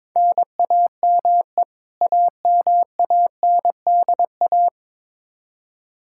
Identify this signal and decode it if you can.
Morse code, heard as a keyed tone, about 22 wpm, 700 Hz. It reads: NAME AMANDA
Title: Practice QSO - Exchange 1 - Beginning - 22wpm